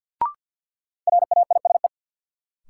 Morse code, heard as a keyed tone, about 48 wpm, 700 Hz. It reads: RAISE
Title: Sentences from Top 1400 Words - New Word Drill Alt - 50wpm